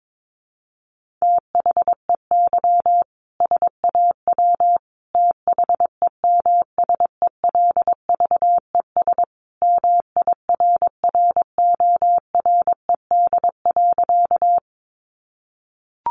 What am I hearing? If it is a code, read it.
Morse code, heard as a keyed tone, about 22 wpm, 700 Hz. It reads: THEY SAW THEMSELVES MIRRORED.
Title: A Sound Of Thunder 22wpm